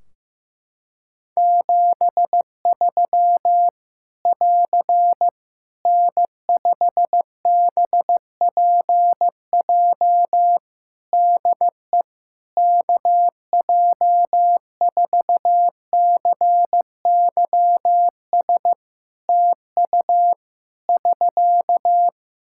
Morse code, heard as a keyed tone, about 16 wpm, 700 Hz. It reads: 73 <AR> N5BPJ DE KJ4CYS TU <SK>
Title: Practice QSO - Exchange 4 - Ending - 15wpm